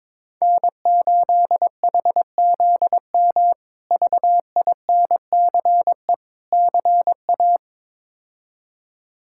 Morse code, heard as a keyed tone, about 22 wpm, 700 Hz. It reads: N8HZM VINCE CA
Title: K1USN SST QSO Practice - 22wpm